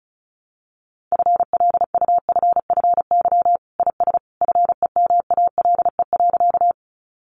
Morse code, heard as a keyed tone, about 34 wpm, 700 Hz. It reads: FLUFFY IS FEMALE.
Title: Easy Phrases 35wpm